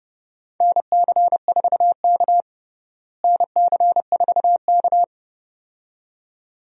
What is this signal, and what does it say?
Morse code, heard as a keyed tone, about 32 wpm, 700 Hz. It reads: NC4K NC4K
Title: USA Call Signs - 4 Characters - Repeated 2x - 30wpm